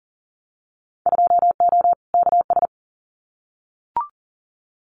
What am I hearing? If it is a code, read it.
Morse code, heard as a keyed tone, about 40 wpm, 700 Hz. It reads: 2O KS
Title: Winter Field Day Exchanges Encore 40wpm